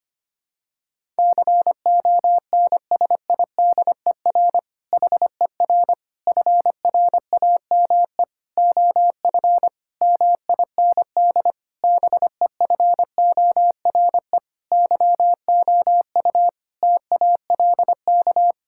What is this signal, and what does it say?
Morse code, heard as a keyed tone, about 25 wpm, 700 Hz. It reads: CONSIDER HER FRAME OF MIND BEFORE YOU TALK
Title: Sentences from Top 1000 Words - Review Part 6 - 25wpm